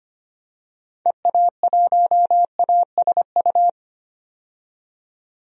Morse code, heard as a keyed tone, about 25 wpm, 700 Hz. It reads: EA1ASU
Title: DXCC-100 Call Signs - 6 Characters - 25wpm